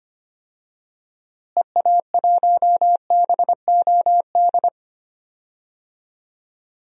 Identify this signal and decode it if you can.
Morse code, heard as a keyed tone, about 25 wpm, 700 Hz. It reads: EA1BOD
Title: DXCC-100 Call Signs - 6 Characters - Repeated 2x - 25wpm